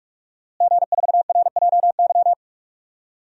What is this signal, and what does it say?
Morse code, heard as a keyed tone, about 43 wpm, 700 Hz. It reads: G4RJY
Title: DXCC-100 Call Signs - 5 Characters - Repeated 3x - 45wpm